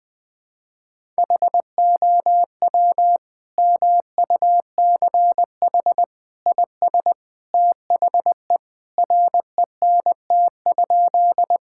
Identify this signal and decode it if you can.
Morse code, heard as a keyed tone, about 20 wpm, 700 Hz. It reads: HOW MUCH IS THE RENT?
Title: Sentences from Top 1000 Words - Review Part 5 - 20wpm